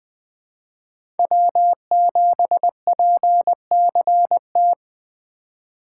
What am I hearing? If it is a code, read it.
Morse code, heard as a keyed tone, about 20 wpm, 700 Hz. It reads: W7PCT
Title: 2023 POTA Activator Call Signs - 20wpm